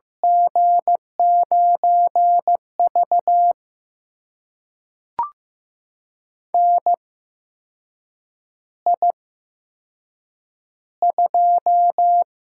Morse code, heard as a keyed tone, about 15 wpm, spaced as for 6 wpm, 700 Hz. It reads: G9V NI2
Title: USA Call Signs - 4 Characters - 8x Character Spacing - 15wpm